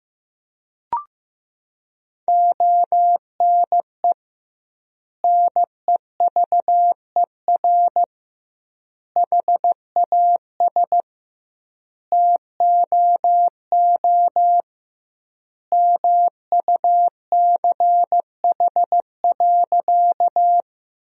Morse code, heard as a keyed tone, about 15 wpm, 700 Hz. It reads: ONE NEVER HAS TOO MUCH.
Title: 2x Word-Spacing - Sentences from Top 300 Words - 15wpm